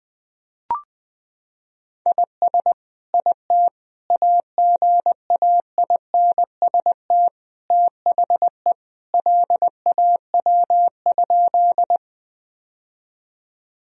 Morse code, heard as a keyed tone, about 20 wpm, 700 Hz. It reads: IS IT AGAINST THE LAW?
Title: Sentences from Top 1000 Words - Review Part 3 - 20wpm